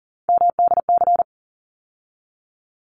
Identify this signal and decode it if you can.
Morse code, heard as a keyed tone, about 40 wpm, 700 Hz. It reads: MDC
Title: ARRL Sections 40wpm